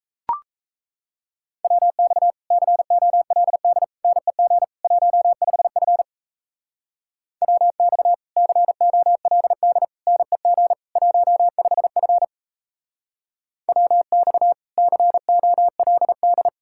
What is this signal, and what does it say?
Morse code, heard as a keyed tone, about 36 wpm, 700 Hz. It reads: WX COLD NEG 15F WX COLD NEG 15F WX COLD